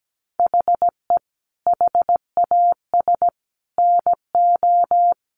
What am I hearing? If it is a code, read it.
Morse code, heard as a keyed tone, about 17 wpm, 700 Hz. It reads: HE HAS NO
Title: Sentences from Top 1200 Words 17wpm